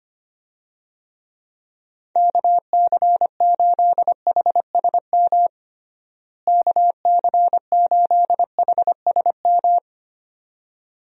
Morse code, heard as a keyed tone, about 25 wpm, 700 Hz. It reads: KC8HSM KC8HSM
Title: USA Call Signs - 6 Characters - Repeated 2x - 25wpm